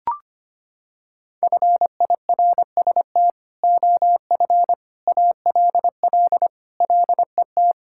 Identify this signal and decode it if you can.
Morse code, heard as a keyed tone, about 25 wpm, 700 Hz. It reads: FIRST OF ALL LET
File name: practice / 25wpm / Sentences with 3-Word Phrases - 25wpm